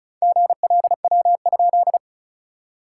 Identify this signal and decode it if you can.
Morse code, heard as a keyed tone, about 34 wpm, 690 Hz. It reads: GLW?
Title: SOTA QSO Practice 35wpm